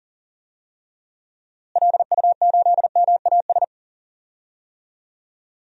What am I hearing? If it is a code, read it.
Morse code, heard as a keyed tone, about 40 wpm, 700 Hz. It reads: LU8MAS
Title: DXCC-100 Call Signs - 6 Characters - Repeated 2x - 40wpm